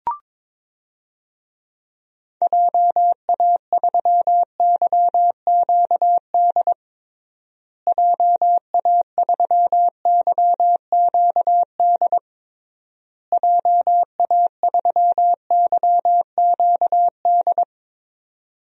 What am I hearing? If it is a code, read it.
Morse code, heard as a keyed tone, about 22 wpm, 700 Hz. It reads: JA3YQD JA3YQD JA3YQD
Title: DXCC-100 Call Signs - 6 Characters - Repeated 3x - 22wpm